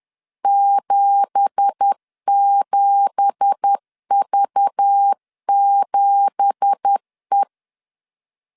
Morse code, heard as a keyed tone, about 10 wpm, 800 Hz. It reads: 77V7E
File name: LEZIONI / 9 2867 10wpm 2733